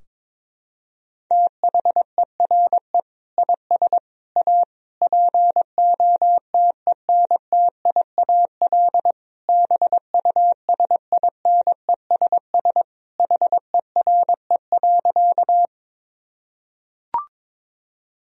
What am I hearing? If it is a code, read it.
Morse code, heard as a keyed tone, about 22 wpm, 700 Hz. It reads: THERE IS A POTENTIAL BUSINESS HERE.
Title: Sentences from Top 1000 Words - Review Part 4 - 22wpm